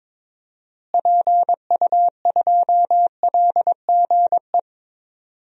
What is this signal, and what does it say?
Morse code, heard as a keyed tone, about 22 wpm, 700 Hz. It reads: PU2LGE
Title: DXCC-100 Call Signs - 6 Characters - Repeated 2x - 22wpm